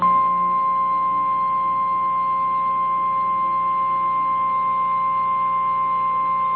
Emergency Alert System (SAME) tones (the 1050 Hz attention tone). It is heard from 1.2 to 5.8 s.
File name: eas / KNCYKBIE / KNCYKBIE